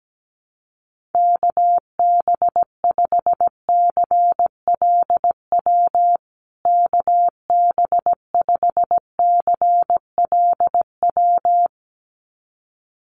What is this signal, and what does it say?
Morse code, heard as a keyed tone, about 17 wpm, 700 Hz. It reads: KB5CLW KB5CLW
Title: USA Call Signs Repeated 2x 17wpm